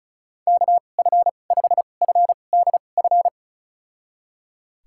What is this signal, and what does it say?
Morse code, heard as a keyed tone, about 36 wpm, spaced as for 25 wpm, 700 Hz. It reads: KF5FDF
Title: USA Call Signs - 6 Characters - 2x Character Spacing - 35wpm